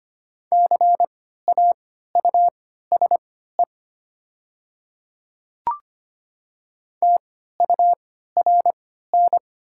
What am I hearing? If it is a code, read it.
Morse code, heard as a keyed tone, about 25 wpm, spaced as for 14 wpm, 700 Hz. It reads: CAUSE TURN
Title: 3x Character Spacing - Top 300 Words - 25wpm